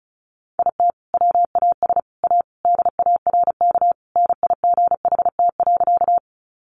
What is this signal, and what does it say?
Morse code, heard as a keyed tone, about 34 wpm, 700 Hz. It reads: IT WAS A DARK NIGHT.
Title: Sentences from Top 500 Words 35wpm